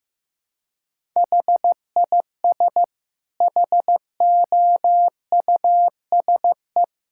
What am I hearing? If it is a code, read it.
Morse code, heard as a keyed tone, about 16 wpm, 700 Hz. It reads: HIS HOUSE
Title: Sentences from Top 1000 Words - Review Part 6 - 15wpm